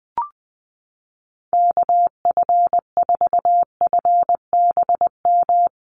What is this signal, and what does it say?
Morse code, heard as a keyed tone, about 20 wpm, 700 Hz. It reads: KF4FBM
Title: USA Call Signs Repeated 2x 20wpm